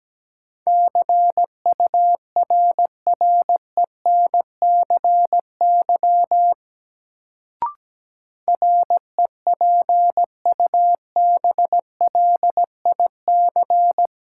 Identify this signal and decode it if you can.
Morse code, heard as a keyed tone, about 17 wpm, 700 Hz. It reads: CURRENCY REPUBLIC
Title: Sentences from Top 1500 Words - New Word Drill Alt - 17wpm